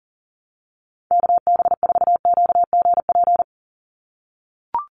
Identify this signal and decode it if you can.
Morse code, heard as a keyed tone, about 37 wpm, 700 Hz. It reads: KB4QGP
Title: USA Call Signs 40wpm